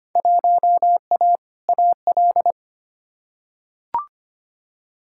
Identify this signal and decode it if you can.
Morse code, heard as a keyed tone, about 25 wpm, 700 Hz. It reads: 1A AL
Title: Field Day Exchanges - 25wpm